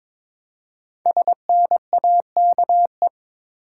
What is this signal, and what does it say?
Morse code, heard as a keyed tone, about 22 wpm, 700 Hz. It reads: SNAKE
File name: 5 Letter Words - First 3 Letters Given - 22wpm